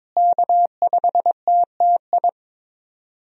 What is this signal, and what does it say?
Morse code, heard as a keyed tone, about 22 wpm, 700 Hz. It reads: K5TTI
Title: USA Call Signs - 5 Characters - Repeated 3x - 22wpm